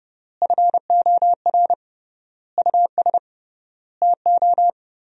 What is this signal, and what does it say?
Morse code, heard as a keyed tone, about 30 wpm, 700 Hz. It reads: FOR US TO
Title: 3x Word-Spacing - Sentences from Top 500 Words - 30wpm